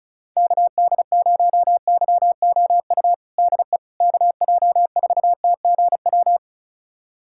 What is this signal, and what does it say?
Morse code, heard as a keyed tone, about 34 wpm, 700 Hz. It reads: KD0YOU DE KJ4TGW
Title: USA Callsign DE Callsign - 6 Characters - 35wpm